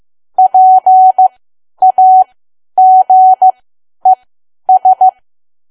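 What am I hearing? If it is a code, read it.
Morse code, heard as a keyed tone, about 15 wpm, spaced as for 10 wpm, 750 Hz. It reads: PAGES
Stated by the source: Morse learning practice at 10 wpm.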